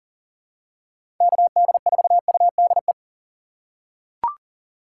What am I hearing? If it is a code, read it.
Morse code, heard as a keyed tone, about 40 wpm, 700 Hz. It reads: KD4UDE
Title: USA Call Signs - 6 Characters - 6x Character Spacing - 40wpm